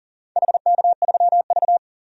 Morse code, heard as a keyed tone, about 38 wpm, 700 Hz. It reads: HK3V
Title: DXCC-100 Call Signs - 4 Characters - 40wpm